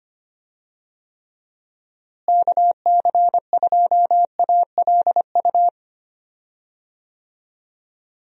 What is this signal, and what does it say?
Morse code, heard as a keyed tone, about 25 wpm, 700 Hz. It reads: KC2ALU
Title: USA Call Signs - 6 Characters - Repeated 2x - 25wpm